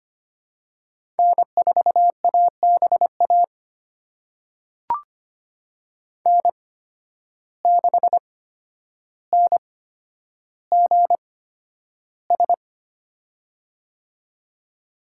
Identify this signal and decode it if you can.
Morse code, heard as a keyed tone, about 25 wpm, spaced as for 11 wpm, 700 Hz. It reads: N4ABA N6NGS
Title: USA Call Signs - 5 Characters - 8x Character Spacing - 25wpm